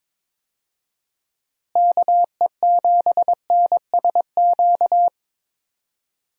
Morse code, heard as a keyed tone, about 22 wpm, 700 Hz. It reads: KE7NSQ
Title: USA Call Signs - 6 Characters - Repeated 3x - 22wpm